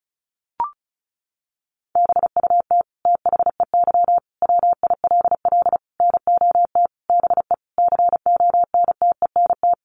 Morse code, heard as a keyed tone, about 34 wpm, 700 Hz. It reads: BUT THEY WILL NOT BE CONTENT
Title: Sentences from Top 900 Words 35wpm